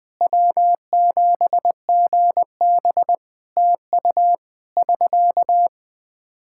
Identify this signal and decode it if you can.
Morse code, heard as a keyed tone, about 20 wpm, 700 Hz. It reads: W7GB TU <SK>
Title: Practice QSO - Exchange 4 - Ending - 20wpm